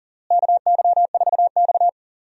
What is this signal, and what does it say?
Morse code, heard as a keyed tone, about 40 wpm, 700 Hz. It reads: KY4X
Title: USA Call Signs - 4 Characters - Repeated 3x - 40wpm